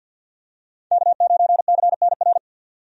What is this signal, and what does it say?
Morse code, heard as a keyed tone, about 46 wpm, 700 Hz. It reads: K9CNR
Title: USA Call Signs - 5 Characters - 50wpm